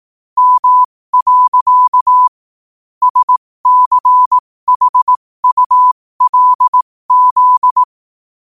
Morse code, heard as a keyed tone, about 18 wpm, spaced as for 15 wpm, 1000 Hz. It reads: M. SCHULZ
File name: qotdfiles / 15wpm